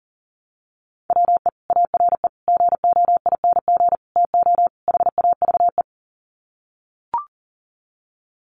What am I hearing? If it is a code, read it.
Morse code, heard as a keyed tone, about 40 wpm, 700 Hz. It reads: WE ARE GOING TO HAVE
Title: Sets of 5 Words Encore 40wpm